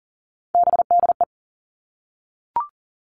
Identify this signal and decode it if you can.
Morse code, heard as a keyed tone, about 41 wpm, 700 Hz. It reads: BDE
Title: Three Challenging Letter-Number Combinations 40wpm